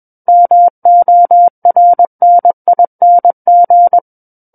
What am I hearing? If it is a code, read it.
Morse code, heard as a keyed tone, about 21 wpm, 700 Hz. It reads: MORNING
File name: Book One / 12-Chapter 12 - 21 WPM